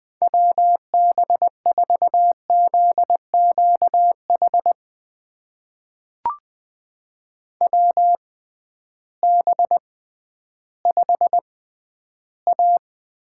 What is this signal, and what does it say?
Morse code, heard as a keyed tone, about 20 wpm, spaced as for 12 wpm, 700 Hz. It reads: WB4ZQH WB5A
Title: USA Call Signs - 6 Characters - 6x Character Spacing - 20wpm